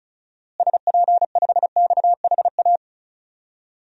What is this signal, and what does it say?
Morse code, heard as a keyed tone, about 34 wpm, 700 Hz. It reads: SP5XHA
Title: DXCC-100 Call Signs - 6 Characters - Repeated 2x - 35wpm